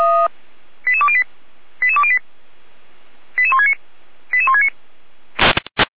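Five-tone selective call (ZVEI), heard as quick runs of five tones, tones of about 70 ms. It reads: 88208 88208 88169 88169
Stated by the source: ZVEI five-tone selective call received by SDR, Feuerwehr Orlamünde (f85360).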